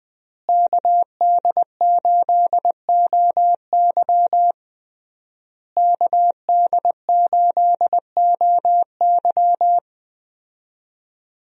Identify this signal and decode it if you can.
Morse code, heard as a keyed tone, about 20 wpm, 700 Hz. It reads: KD8OY KD8OY
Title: USA Call Signs - 5 Characters - Repeated 3x - 20wpm